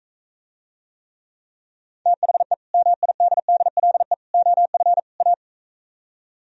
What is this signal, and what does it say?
Morse code, heard as a keyed tone, about 43 wpm, 700 Hz. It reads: THE MIDDLE OF A